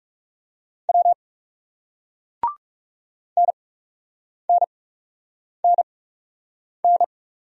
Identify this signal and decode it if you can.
Morse code, heard as a keyed tone, about 38 wpm, spaced as for 8 wpm, 700 Hz. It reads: W NNNN